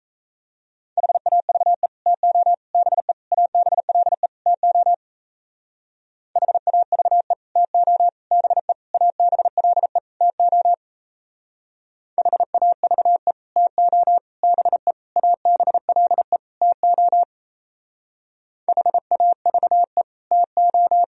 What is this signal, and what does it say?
Morse code, heard as a keyed tone, about 33 wpm, 700 Hz. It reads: HAVE TO BE ABLE TO HAVE TO BE ABLE TO HAVE TO BE ABLE TO HAVE TO
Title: Speed-Racing - Sets of 5 Words 28wpm